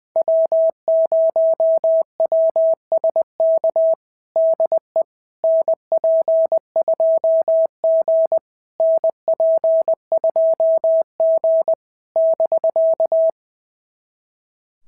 Morse code, heard as a keyed tone, about 20 wpm, 650 Hz. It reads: W0WSK DE NP2G NP2G <BK>
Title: SOTA QSO Practice 20wpm